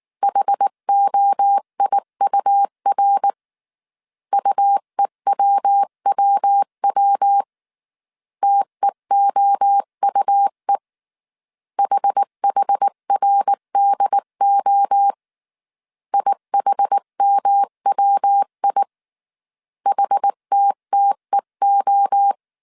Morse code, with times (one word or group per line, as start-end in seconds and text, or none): 0.2-3.3 HOIUR
4.3-7.4 UEWWW
8.4-10.8 TEOUE
11.8-15.1 HHRDO
16.1-18.8 IHMWI
19.9-22.3 HTTEO